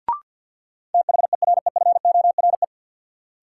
Morse code, heard as a keyed tone, about 50 wpm, 700 Hz. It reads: THEREFORE
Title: Sentences from Top 800 Words - New Word Drill - 50wpm